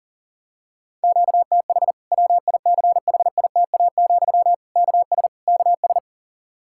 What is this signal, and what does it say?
Morse code, heard as a keyed tone, about 40 wpm, 700 Hz. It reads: QTH WICHITA, KS KS
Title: K1USN SST QSO Practice 40wpm